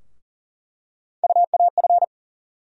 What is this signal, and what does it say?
Morse code, heard as a keyed tone, about 40 wpm, 700 Hz. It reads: UAF
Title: USA Call Sign Postfixes - 3 Letters 40wpm